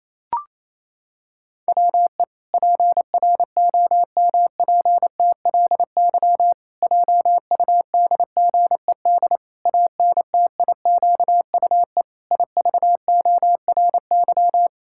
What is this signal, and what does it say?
Morse code, heard as a keyed tone, about 28 wpm, 700 Hz. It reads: WE PROMPTLY JUDGED ANTIQUE IVORY
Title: Pangram Challenge v2 - 28wpm